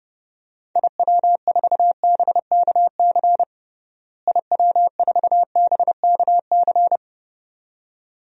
Morse code, heard as a keyed tone, about 32 wpm, 700 Hz. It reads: IW4BKC IW4BKC
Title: DXCC-100 Call Signs - 6 Characters - Repeated 2x - 30wpm